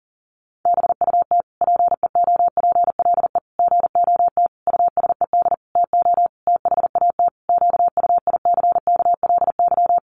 Morse code, heard as a keyed tone, about 40 wpm, 700 Hz. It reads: BUT PEOPLE GOT USED TO THAT QUICKLY
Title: Sentences from Top 700 Words 40wpm